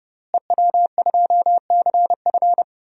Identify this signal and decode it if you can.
Morse code, heard as a keyed tone, about 32 wpm, 700 Hz. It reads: EW2CF
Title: DXCC-100 Call Signs - 5 Characters - Repeated 3x - 30wpm